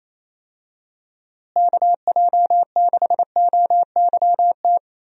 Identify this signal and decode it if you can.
Morse code, heard as a keyed tone, about 28 wpm, 700 Hz. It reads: KJ6OYT
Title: USA Call Signs - 6 Characters - Repeated 2x - 28wpm